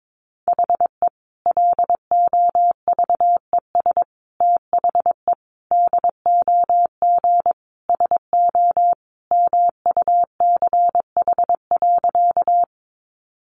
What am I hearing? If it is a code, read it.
Morse code, heard as a keyed tone, about 22 wpm, 700 Hz. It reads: HE LOVES THE DOG SO MUCH.
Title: Additional Easy Phrases 22wpm